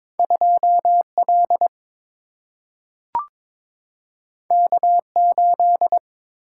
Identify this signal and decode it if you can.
Morse code, heard as a keyed tone, about 22 wpm, 700 Hz. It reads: 2 L K 8
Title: Single-Letter Single-Number 22wpm